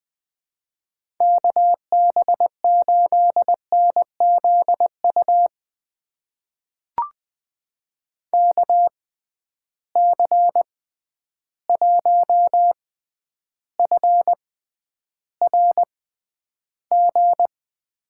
Morse code, heard as a keyed tone, about 20 wpm, spaced as for 10 wpm, 700 Hz. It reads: KB8NZU KC1FRG
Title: USA Call Signs - 6 Characters - 6x Character Spacing - 20wpm